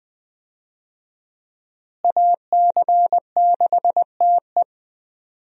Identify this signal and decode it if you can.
Morse code, heard as a keyed tone, about 21 wpm, 700 Hz. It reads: AC6TE
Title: USA Call Signs - 5 Characters - Repeated 2x - 20wpm